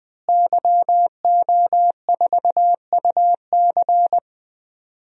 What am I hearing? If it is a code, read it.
Morse code, heard as a keyed tone, about 21 wpm, 700 Hz. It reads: YO4UC
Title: DXCC-100 Call Signs - 5 Characters - Repeated 2x - 20wpm